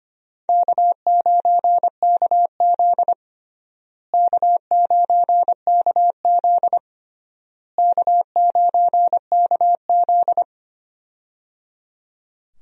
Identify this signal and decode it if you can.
Morse code, heard as a keyed tone, about 25 wpm, 700 Hz. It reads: K9KZ K9KZ K9KZ
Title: USA Call Signs - 4 Characters - Repeated 3x - 25wpm